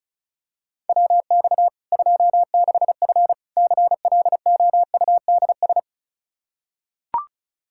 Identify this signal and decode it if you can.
Morse code, heard as a keyed tone, about 36 wpm, 700 Hz. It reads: WX 26F CLOUDS
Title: Practice QSO - Exchange 2 - Info - 35wpm